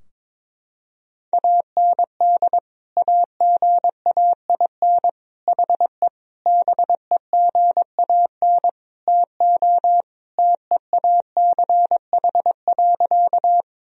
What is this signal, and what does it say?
Morse code, heard as a keyed tone, about 22 wpm, 700 Hz. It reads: AND AGAIN HE BEGAN TO TEACH.
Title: Sentences from Top 1000 Words - Review Part 2 - 22wpm